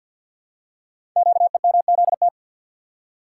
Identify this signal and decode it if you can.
Morse code, heard as a keyed tone, about 46 wpm, 700 Hz. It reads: QEMZT
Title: Code Groups - 5 Letters - 50wpm